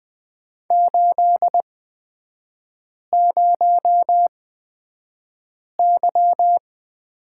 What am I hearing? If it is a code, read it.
Morse code, heard as a keyed tone, about 20 wpm, spaced as for 5 wpm, 700 Hz. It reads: 80Y
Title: 250 Lesson 40e - BK - Three Characters 20wpm